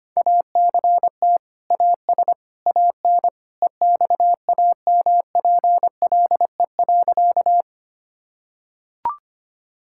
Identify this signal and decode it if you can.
Morse code, heard as a keyed tone, about 25 wpm, 700 Hz. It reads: ACT AS AN EXAMPLE.
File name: practice / 25wpm / Sentences with 2-Word Phrases - 25wpm